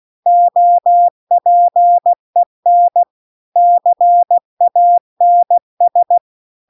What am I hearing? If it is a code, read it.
Morse code, heard as a keyed tone, about 16 wpm, 700 Hz. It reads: OPEN CANS